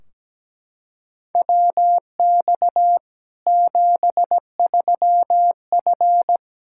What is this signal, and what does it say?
Morse code, heard as a keyed tone, about 17 wpm, 700 Hz. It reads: WX 73F
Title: Practice QSO - Exchange 2 - Info - 17wpm